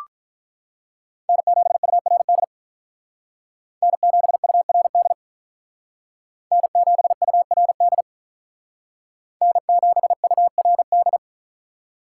Morse code, heard as a keyed tone, about 43 wpm, 700 Hz. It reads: N7URD N7URD N7URD N7URD
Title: Speed Racing - USA 1x3 Call Signs 35wpm